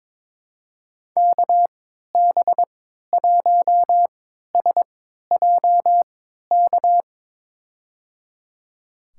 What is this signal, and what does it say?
Morse code, heard as a keyed tone, about 22 wpm, spaced as for 13 wpm, 700 Hz. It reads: KB1SJK
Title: USA Call Signs - 6 Characters - 3x Character Spacing - 22wpm